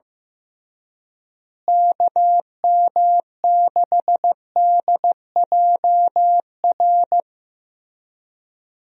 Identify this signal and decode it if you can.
Morse code, heard as a keyed tone, about 15 wpm, 700 Hz. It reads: KM6DJR
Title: USA Call Signs - 6 Characters - Repeated 3x - 15wpm